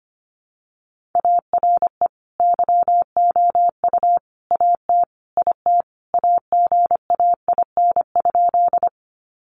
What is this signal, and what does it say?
Morse code, heard as a keyed tone, about 25 wpm, 700 Hz. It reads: ARE YOU AT IT AGAIN?
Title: Sentences from Top 200 Words 25wpm